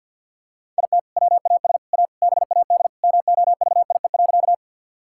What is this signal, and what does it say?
Morse code, heard as a keyed tone, about 50 wpm, 700 Hz. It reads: IT WAS A BAD MOVIE.
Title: Sentences from Top 1000 Words - Review Part 5 - 50wpm